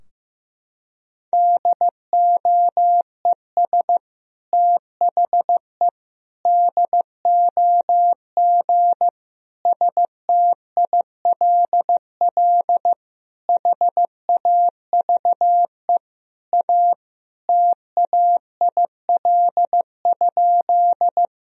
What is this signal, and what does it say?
Morse code, heard as a keyed tone, about 16 wpm, 700 Hz. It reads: DOES THE DOG STILL HAVE A TAIL?
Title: Sentences from Top 1000 Words - Review Part 3 - 15wpm